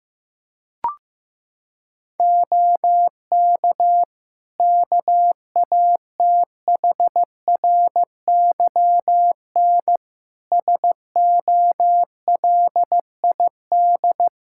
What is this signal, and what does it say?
Morse code, heard as a keyed tone, about 15 wpm, 700 Hz. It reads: OK KATHRYN SOLID
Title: K1USN SST QSO Practice 15wpm